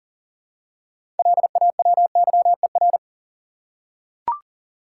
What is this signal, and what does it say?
Morse code, heard as a keyed tone, about 40 wpm, 700 Hz. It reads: LAWYER
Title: Six Letter Words - 40wpm